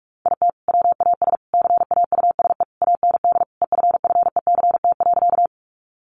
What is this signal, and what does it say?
Morse code, heard as a keyed tone, about 44 wpm, 700 Hz. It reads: IT WAS CAUSE AND EFFECT.
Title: Sentences from Top 1000 Words 45wpm